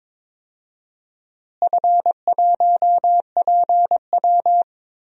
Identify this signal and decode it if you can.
Morse code, heard as a keyed tone, about 22 wpm, 700 Hz. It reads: F1PW
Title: DXCC-100 Call Signs - 4 Characters - Repeated 2x - 22wpm